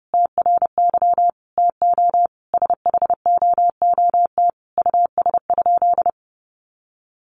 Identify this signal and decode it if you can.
Morse code, heard as a keyed tone, about 30 wpm, 700 Hz. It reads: TRY TO SHOOT US?
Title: Sentences from Top 1200 Words 30wpm